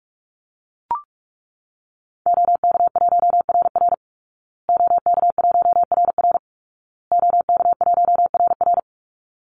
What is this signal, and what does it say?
Morse code, heard as a keyed tone, about 44 wpm, 700 Hz. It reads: OK1RR OK1RR OK1RR
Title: Call Signs Repeated 2x - 45wpm